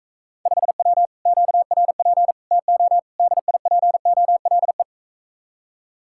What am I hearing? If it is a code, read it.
Morse code, heard as a keyed tone, about 45 wpm, 700 Hz. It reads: 5W QRP TO DIPOLE